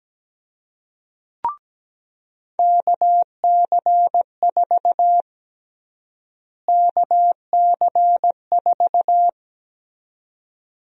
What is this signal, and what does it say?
Morse code, heard as a keyed tone, about 17 wpm, 700 Hz. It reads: KC4 KC4
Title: USA Call Sign Prefixes - Repeated 2x - 17wpm